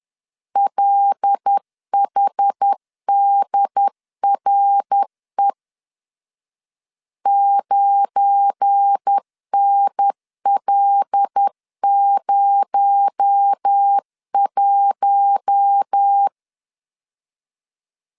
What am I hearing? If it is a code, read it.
Morse code, heard as a keyed tone, about 10 wpm, 800 Hz. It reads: LHDRE 9NL01
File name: LEZIONI / 7 GF9Z / 7 GF9 10wpm 2747